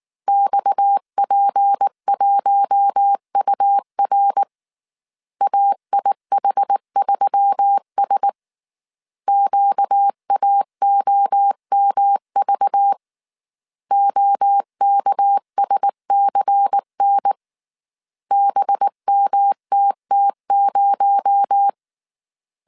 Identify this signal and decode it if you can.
Morse code, heard as a keyed tone, about 19 wpm, 800 Hz. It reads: XP1UR AIH3S QAOMV OKSCN BMTT0
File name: CW / DET / 1 6 1500C S A 18wpm 2255